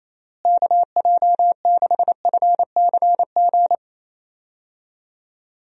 Morse code, heard as a keyed tone, about 28 wpm, 700 Hz. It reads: KJ6FCG
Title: USA Call Signs - 6 Characters - Repeated 3x - 28wpm